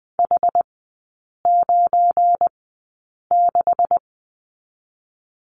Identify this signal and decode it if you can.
Morse code, heard as a keyed tone, about 20 wpm, spaced as for 8 wpm, 690 Hz. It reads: H96
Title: 172 Lesson 28e - Number 9 - Three Characters 20wpm